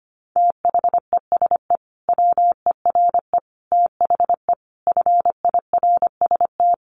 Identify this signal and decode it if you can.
Morse code, heard as a keyed tone, about 25 wpm, 700 Hz. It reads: THESE WERE THE FIRST
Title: Sentences from Top 300 Words 25wpm